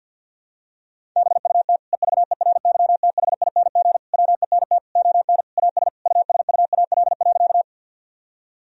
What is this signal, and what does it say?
Morse code, heard as a keyed tone, about 51 wpm, 700 Hz. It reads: BUT EVERYTHING WENT ON AS USUAL.
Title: Sentences from Top 1000 Words - Review Part 2 - 50wpm